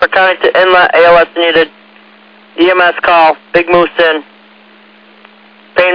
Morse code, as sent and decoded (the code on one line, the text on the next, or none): none